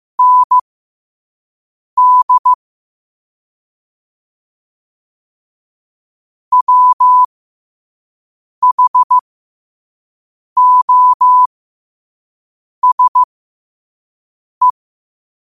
Morse code, heard as a keyed tone, about 15 wpm, spaced as for 5 wpm, 1000 Hz. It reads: ND WHOSE
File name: qotdfiles / 5wpm